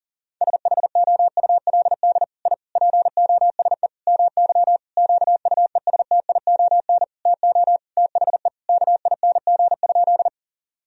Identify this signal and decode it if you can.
Morse code, heard as a keyed tone, about 40 wpm, 700 Hz. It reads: SHOULD I POSE MY QUESTION TO THE KING?
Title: Sentences from Top 1000 Words - Review Part 2 - 40wpm